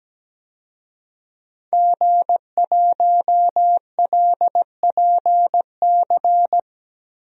Morse code, heard as a keyed tone, about 17 wpm, 700 Hz. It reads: G1LPC
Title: DXCC-100 Call Signs - 5 Characters - Repeated 2x - 17wpm